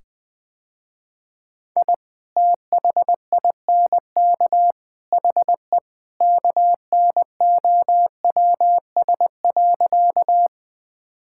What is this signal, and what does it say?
Morse code, heard as a keyed tone, about 21 wpm, 700 Hz. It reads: I THINK HE KNOWS.
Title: Sentences with 3-Word Phrases - Spoken Beginning - 20wpm